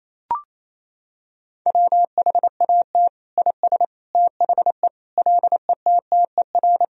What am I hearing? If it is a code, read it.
Morse code, heard as a keyed tone, about 28 wpm, 700 Hz. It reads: WHAT IS THE LETTER
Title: Sentences from Top 1000 Words - Review Part 2 - 28wpm